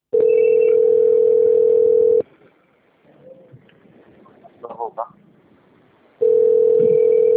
Ringback tone (North America), 440 + 480 Hz.